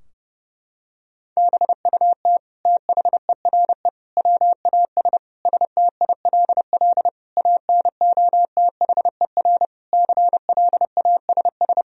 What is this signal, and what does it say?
Morse code, heard as a keyed tone, about 32 wpm, 700 Hz. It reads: BUT THERE WAS STILL ANOTHER CLASS
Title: Sentences from Top 1000 Words - Review Part 2 - 30wpm